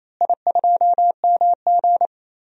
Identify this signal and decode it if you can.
Morse code, heard as a keyed tone, about 28 wpm, 700 Hz. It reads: I2MG
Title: DXCC-100 Call Signs - 4 Characters - Repeated 2x - 28wpm